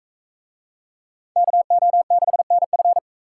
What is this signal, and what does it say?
Morse code, heard as a keyed tone, about 41 wpm, 700 Hz. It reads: KO6NF